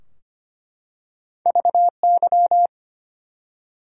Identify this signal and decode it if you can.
Morse code, heard as a keyed tone, about 25 wpm, 700 Hz. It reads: VY
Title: Top 90 QSO Elements 25wpm